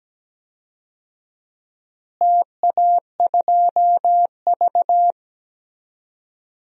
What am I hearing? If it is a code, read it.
Morse code, heard as a keyed tone, about 17 wpm, 700 Hz. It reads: TA2V
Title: DXCC-100 Call Signs - 4 Characters - Repeated 3x - 17wpm